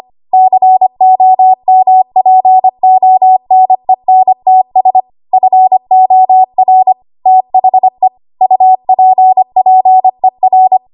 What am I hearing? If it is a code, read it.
Morse code, heard as a keyed tone, about 25 wpm, 750 Hz. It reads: COMPONENTS FOR THE UPPER
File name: Morse / 25WPM